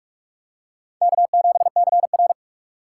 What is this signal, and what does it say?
Morse code, heard as a keyed tone, about 46 wpm, 700 Hz. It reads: K7CR
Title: USA Call Signs - 4 Characters - 6x Character Spacing - 45wpm